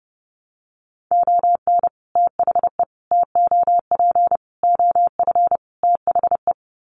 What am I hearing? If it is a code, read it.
Morse code, heard as a keyed tone, about 30 wpm, 700 Hz. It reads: ON THE TOP OF THE
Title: Sets of 5 Words Encore 30wpm